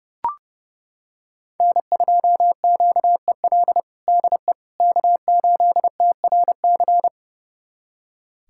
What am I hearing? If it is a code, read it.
Morse code, heard as a keyed tone, about 32 wpm, 700 Hz. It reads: N2QEL DE K8TRC
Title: USA Callsign DE Callsign - 5 Characters - 30wpm